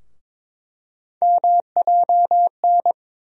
Morse code, heard as a keyed tone, about 22 wpm, 700 Hz. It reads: MJN
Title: USA Call Sign Postfixes - 3 Letters 22wpm